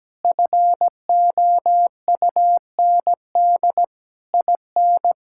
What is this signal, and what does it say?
Morse code, heard as a keyed tone, about 17 wpm, 700 Hz. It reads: FOUND IN
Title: Sentences from Top 1000 Words - Review Part 6 - 17wpm